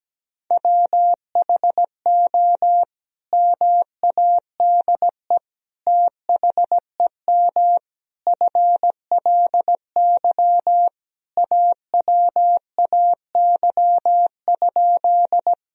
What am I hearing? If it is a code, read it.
Morse code, heard as a keyed tone, about 17 wpm, 700 Hz. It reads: WHO MADE THEM FLY AWAY?
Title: Sentences from Top 1000 Words - Review Part 2 - 17wpm